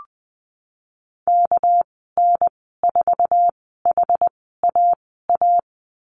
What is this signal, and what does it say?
Morse code, heard as a keyed tone, about 20 wpm, spaced as for 15 wpm, 700 Hz. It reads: KN4HAA
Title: 245 Lesson 39g - Letter z - Call Signs 20wpm